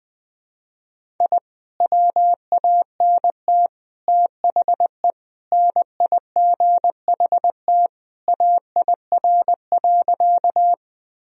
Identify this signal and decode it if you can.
Morse code, heard as a keyed tone, about 20 wpm, 700 Hz. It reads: I WANT THE NIGHT AIR.
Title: Sentences from Top 1000 Words - Review Part 1 - 20wpm